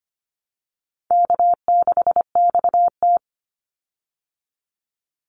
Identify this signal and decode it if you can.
Morse code, heard as a keyed tone, about 25 wpm, 700 Hz. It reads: K6XT
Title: Easy Call Signs Repeated 3x - 25wpm